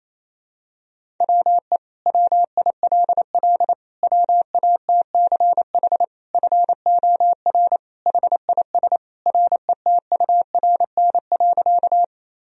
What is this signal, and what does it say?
Morse code, heard as a keyed tone, about 28 wpm, 700 Hz. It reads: WE WILL WATCH FOR HIS RETURN.
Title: Sentences from Top 1000 Words - Review Part 1 - 28wpm